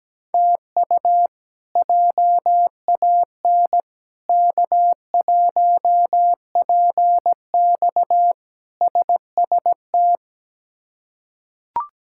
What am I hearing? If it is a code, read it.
Morse code, heard as a keyed tone, about 17 wpm, 700 Hz. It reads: TU JAN K1PX SST
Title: K1USN SST QSO Practice - 17wpm